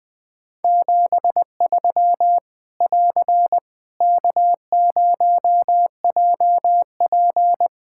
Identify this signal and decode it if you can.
Morse code, heard as a keyed tone, about 20 wpm, 700 Hz. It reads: 73 <AR> K0JP
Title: Practice QSO - Exchange 4 - Ending - 20wpm